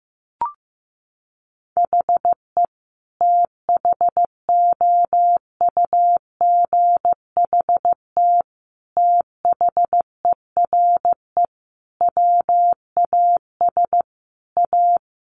Morse code, heard as a keyed tone, about 15 wpm, 700 Hz. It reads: HE THOUGHT THERE WAS A
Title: Sentences from Top 1300 Words 15wpm